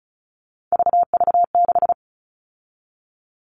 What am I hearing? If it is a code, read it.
Morse code, heard as a keyed tone, about 34 wpm, 700 Hz. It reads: VV6
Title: All Cut Numbers - 3-Digit 35wpm